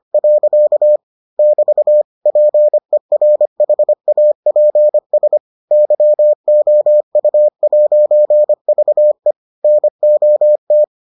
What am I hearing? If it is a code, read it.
Morse code, heard as a keyed tone, about 25 wpm, 600 Hz. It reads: . <BT> PERHAPS YOU'VE NOT